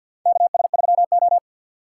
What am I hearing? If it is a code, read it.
Morse code, heard as a keyed tone, about 50 wpm, 700 Hz. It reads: KS3O
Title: USA Call Signs - 4 Characters - 3x Character Spacing - 50wpm